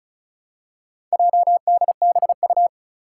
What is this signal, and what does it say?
Morse code, heard as a keyed tone, about 36 wpm, 700 Hz. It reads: JDBU
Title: Four Challenging Letter-Number Combinations 35wpm